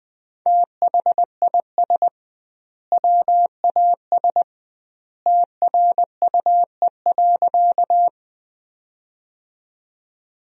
Morse code, heard as a keyed tone, about 21 wpm, 700 Hz. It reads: THIS WAS TRUE.
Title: 2x Word-Spacing - Sentences from Top 400 Words - 20wpm